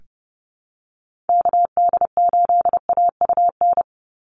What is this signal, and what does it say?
Morse code, heard as a keyed tone, about 30 wpm, 700 Hz. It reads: KD8AUN
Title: USA Call Signs 30wpm